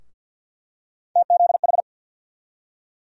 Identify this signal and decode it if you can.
Morse code, heard as a keyed tone, about 52 wpm, 700 Hz. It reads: TZH
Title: USA Call Sign Postfixes - 3 Letters 50wpm